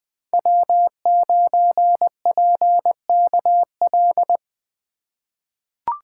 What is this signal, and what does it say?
Morse code, heard as a keyed tone, about 20 wpm, 700 Hz. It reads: W9PKL
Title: USA Call Signs - 5 Characters - 2x Character Spacing - 20wpm